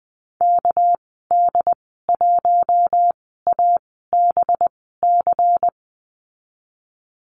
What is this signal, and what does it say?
Morse code, heard as a keyed tone, about 19 wpm, spaced as for 14 wpm, 700 Hz. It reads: KD1ABC
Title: 207 Lesson 33g - Letter j - Call Signs 20wpm